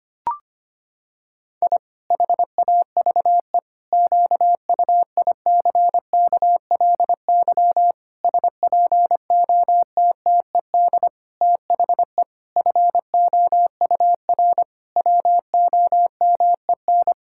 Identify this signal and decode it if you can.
Morse code, heard as a keyed tone, about 25 wpm, 700 Hz. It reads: I HAVE QUICKLY SPOTTED THE FOUR WOMEN
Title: Pangram Challenge v2 - 25wpm